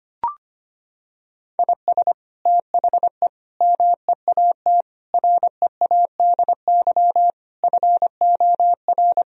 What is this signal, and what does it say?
Morse code, heard as a keyed tone, about 25 wpm, 700 Hz. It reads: IS THE MEAT READY FOR
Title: Sentences from Top 1000 Words - Review Part 3 - 25wpm